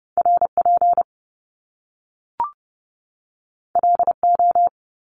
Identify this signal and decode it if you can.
Morse code, heard as a keyed tone, about 30 wpm, 700 Hz. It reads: RP LO
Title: Top Two Letter Combinations in Words 30wpm